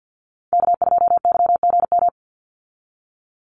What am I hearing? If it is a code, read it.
Morse code, heard as a keyed tone, about 50 wpm, 700 Hz. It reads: K2YGM
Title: Call Signs Repeated 3x - 50wpm